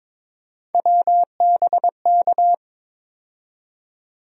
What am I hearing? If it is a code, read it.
Morse code, heard as a keyed tone, about 22 wpm, 700 Hz. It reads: WBK
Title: USA Call Sign Postfixes - 3 Letters 22wpm